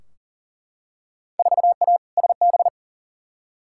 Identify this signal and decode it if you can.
Morse code, heard as a keyed tone, about 40 wpm, 700 Hz. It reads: 4A SB
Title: Field Day Exchanges Encore - 40wpm